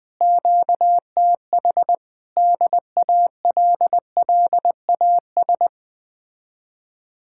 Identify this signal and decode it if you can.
Morse code, heard as a keyed tone, about 20 wpm, 700 Hz. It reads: QTH DALLAS